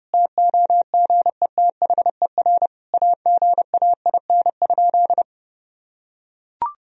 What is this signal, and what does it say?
Morse code, heard as a keyed tone, about 30 wpm, 700 Hz. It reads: TOGETHER AGAIN?
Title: Sentences from Top 1000 Words - Review Part 4 - 30wpm